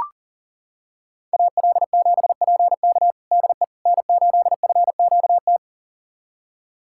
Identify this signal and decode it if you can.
Morse code, heard as a keyed tone, about 40 wpm, 700 Hz. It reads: AL7PK DE N8FQT
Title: USA Callsign DE Callsign - 5 Characters - 40wpm